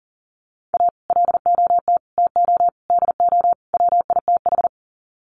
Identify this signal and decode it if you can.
Morse code, heard as a keyed tone, about 40 wpm, 700 Hz. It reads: A LOT TO DO WITH
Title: Sets of 5 Words 40wpm